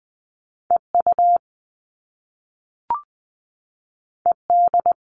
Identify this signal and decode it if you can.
Morse code, heard as a keyed tone, about 20 wpm, 700 Hz. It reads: EU ED
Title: All Cut Numbers - 2-Digit 20wpm